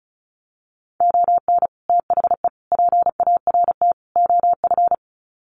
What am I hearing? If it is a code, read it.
Morse code, heard as a keyed tone, about 34 wpm, 700 Hz. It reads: ON THE PART OF
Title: Sets of 4 Words 35wpm